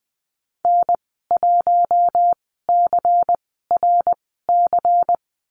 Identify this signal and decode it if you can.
Morse code, heard as a keyed tone, about 20 wpm, spaced as for 14 wpm, 700 Hz. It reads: N1CRC
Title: 154 Lesson 25g - Letter p - Call Signs 20wpm